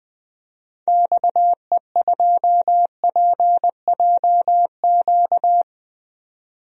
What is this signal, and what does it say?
Morse code, heard as a keyed tone, about 20 wpm, 700 Hz. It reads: XE2PJQ
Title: DXCC-100 Call Signs - 6 Characters - 20wpm